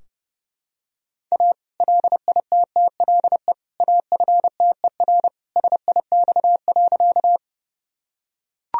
Morse code, heard as a keyed tone, about 32 wpm, 700 Hz. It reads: A LITTLE AFTER SIX.
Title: Sentences from Top 1000 Words - Review Part 2 - 30wpm